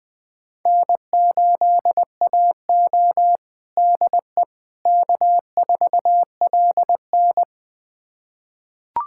Morse code, heard as a keyed tone, about 21 wpm, 700 Hz. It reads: N8AO DE K4LN
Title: USA Callsign DE Callsign - 4 Characters - 20wpm